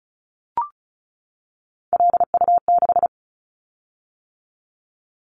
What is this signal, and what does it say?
Morse code, heard as a keyed tone, about 36 wpm, 700 Hz. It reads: LU6
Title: Callsign Prefixes for Top 50 DXCC - 35wpm